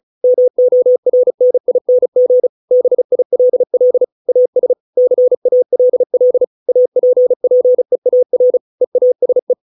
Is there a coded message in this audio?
Yes: Morse code from 0.2 to 9.5 s, heard as a keyed tone, about 34 wpm, 500 Hz. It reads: MORNING BILL AS CALL APPEAR EASE